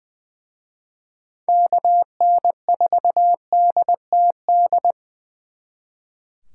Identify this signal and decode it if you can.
Morse code, heard as a keyed tone, about 24 wpm, 700 Hz. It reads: KN4DTD
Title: Speed Racing - USA 2x3 Call Signs 20wpm